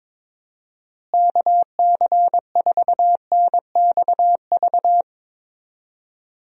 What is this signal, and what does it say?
Morse code, heard as a keyed tone, about 21 wpm, 700 Hz. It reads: KC4NXV
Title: USA Call Signs - 6 Characters - Repeated 3x - 22wpm